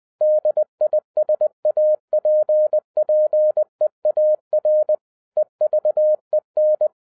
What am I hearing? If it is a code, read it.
Morse code, heard as a keyed tone, about 20 wpm, 610 Hz. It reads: DISAPPEAR EVEN